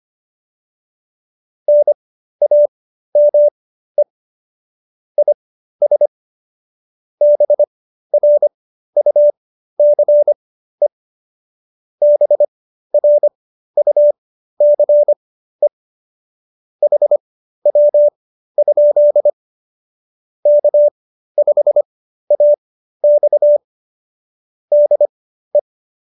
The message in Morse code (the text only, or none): NAME IS BRUCE BRUCE HW? K5AX DE